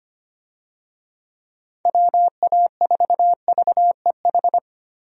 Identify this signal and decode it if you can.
Morse code, heard as a keyed tone, about 25 wpm, 700 Hz. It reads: WA4VEH
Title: USA Call Signs - 6 Characters - Repeated 2x - 25wpm